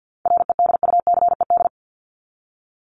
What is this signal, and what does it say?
Morse code, heard as a keyed tone, about 50 wpm, 700 Hz. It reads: REDUCED
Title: ED Suffix - Top 100 Words with Base Words 50wpm